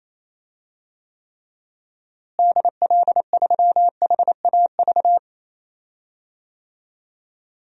Morse code, heard as a keyed tone, about 28 wpm, 700 Hz. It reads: DL3HAV
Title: DXCC-100 Call Signs - 6 Characters - Repeated 2x - 28wpm